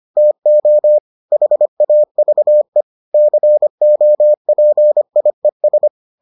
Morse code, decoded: TO HAVE COPIES

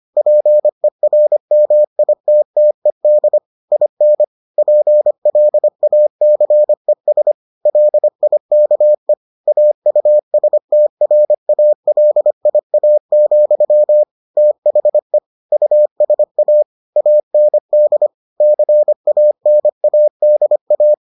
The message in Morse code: PERMITTED IN PLACES LIKE AUSTRALIA, THE USA AND CANADA

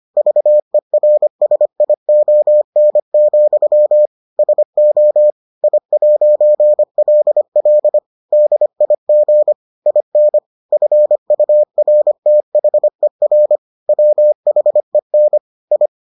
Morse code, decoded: VERSION, SO I'LL DIG IN FURTHER WHEN I

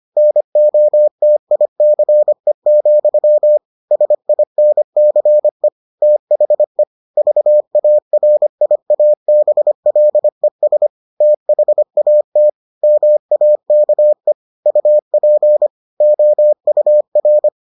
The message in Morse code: NOTICE, SINCE THE VARIABLES THAT MAKE UP OUR

